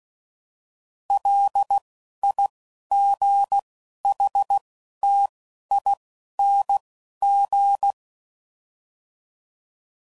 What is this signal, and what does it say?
Morse code, heard as a keyed tone, about 16 wpm, spaced as for 11 wpm, 800 Hz. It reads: LIGHTING